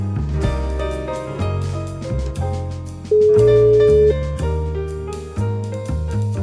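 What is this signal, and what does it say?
Ringback tone (Europe, 425 Hz).
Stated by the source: O2 Czech Republic welcome melody (Blues a Jazz category) that callers hear instead of ringing before the call is answered.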